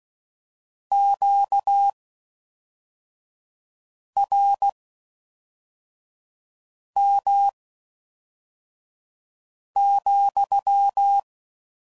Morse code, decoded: QRM,

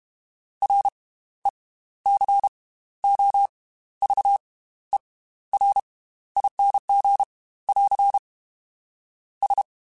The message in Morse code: RECOVERING <AR> S

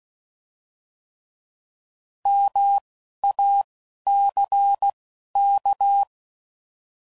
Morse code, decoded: MACK

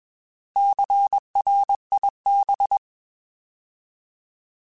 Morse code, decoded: CRIB

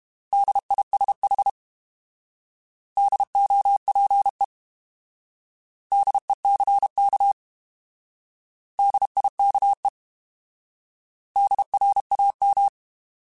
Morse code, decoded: DISH DOPE DECK DIKE DRAM

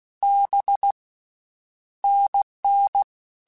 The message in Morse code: B NN

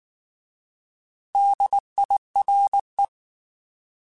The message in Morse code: DIRE

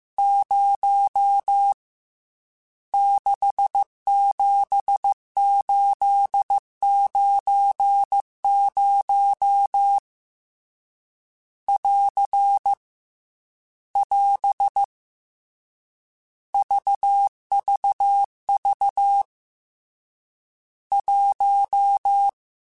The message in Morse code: 0 67890 <AR> <AS> VVV 1

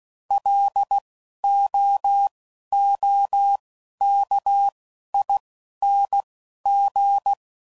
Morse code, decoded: LOOKING

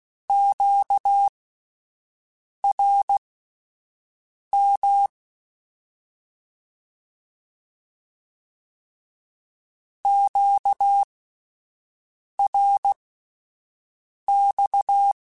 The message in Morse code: QRM QRX

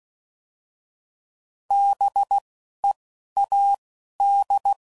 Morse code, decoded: B E A D

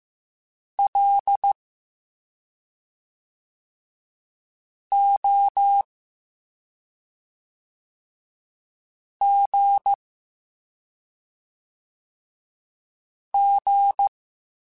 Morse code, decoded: L O G G

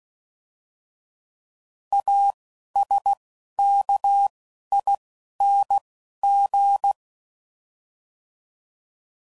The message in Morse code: A S K I N G

